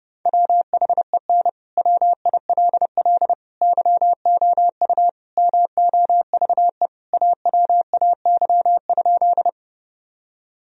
Morse code, heard as a keyed tone, about 32 wpm, 700 Hz. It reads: WHEN WILL YOU MOVE AWAY?
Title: Sentences from Top 1000 Words - Review Part 1 - 30wpm